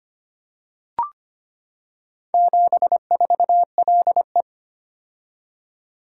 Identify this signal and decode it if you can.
Morse code, heard as a keyed tone, about 25 wpm, 690 Hz. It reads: 74LE
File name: practice / 25wpm / Four Letter-Number Combinations - 25wpm